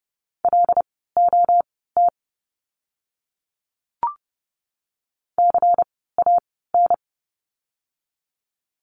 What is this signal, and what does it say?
Morse code, heard as a keyed tone, about 30 wpm, spaced as for 17 wpm, 700 Hz. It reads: LOT CAN
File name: course / 30:10wpm / 115 Lesson 19f - Letter w - Words 30wpm